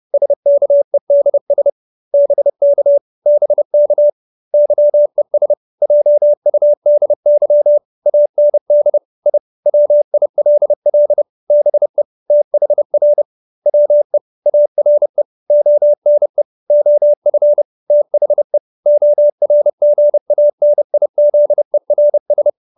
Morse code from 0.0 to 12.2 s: SKEDS BK BK YES JUDY AND I WILL BE

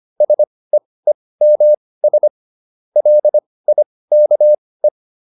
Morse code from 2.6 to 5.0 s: LIKE

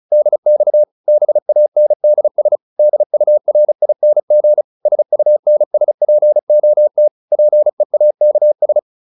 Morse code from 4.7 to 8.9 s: SUNSPOT PEAKS

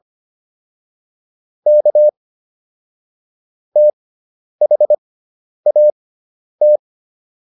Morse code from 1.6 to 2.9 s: K

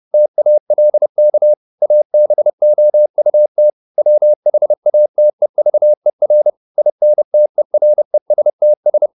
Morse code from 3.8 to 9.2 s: WHATEVER INTERESTS

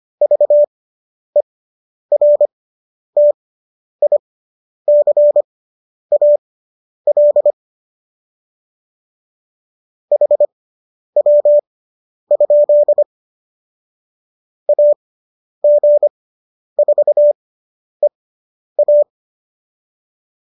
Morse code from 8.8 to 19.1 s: HW? AG4EA

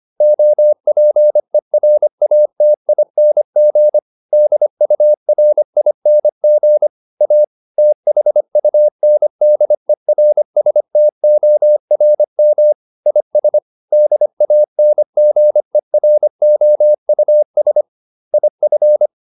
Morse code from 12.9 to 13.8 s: IS

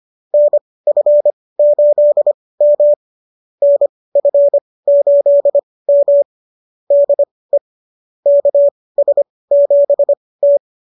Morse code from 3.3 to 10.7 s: NF8M DE KS7T

